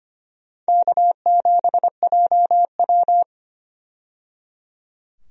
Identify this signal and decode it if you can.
Morse code, heard as a keyed tone, about 25 wpm, 700 Hz. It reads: K7JW
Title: USA Call Signs - 4 Characters - 25wpm